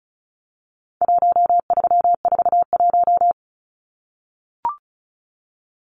Morse code, heard as a keyed tone, about 34 wpm, 700 Hz. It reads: 1341
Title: Numbers - 4-Digit 35wpm